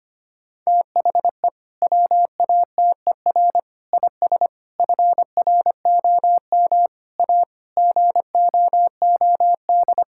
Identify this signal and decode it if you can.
Morse code, heard as a keyed tone, about 25 wpm, 700 Hz. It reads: THE WATER IS FROM A GOOD
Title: Sentences from Top 1000 Words - Review Part 5 - 25wpm